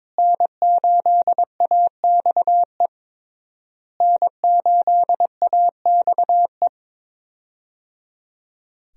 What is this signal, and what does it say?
Morse code, heard as a keyed tone, about 23 wpm, 700 Hz. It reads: N8AXE N8AXE
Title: USA Call Signs - 5 Characters - Repeated 3x - 22wpm